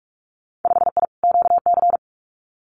Morse code, heard as a keyed tone, about 46 wpm, 700 Hz. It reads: 5I QC
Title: Winter Field Day Exchanges Encore 45wpm